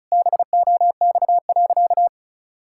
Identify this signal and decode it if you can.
Morse code, heard as a keyed tone, about 36 wpm, 700 Hz. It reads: BOX.
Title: Pangram Challenge v2 - 35wpm